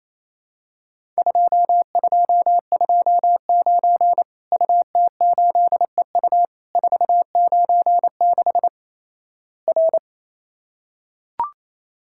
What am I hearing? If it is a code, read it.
Morse code, heard as a keyed tone, about 28 wpm, 700 Hz. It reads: 2229 UT8EU 496 R
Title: Worked All Europe DX Contest - QTC Exchange - 28wpm